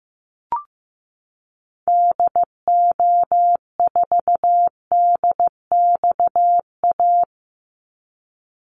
Morse code, heard as a keyed tone, about 15 wpm, 700 Hz. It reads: DO4DXA
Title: Call Signs Repeated 3x - 15wpm